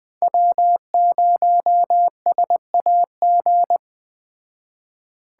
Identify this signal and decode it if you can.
Morse code, heard as a keyed tone, about 20 wpm, 700 Hz. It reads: W0SAG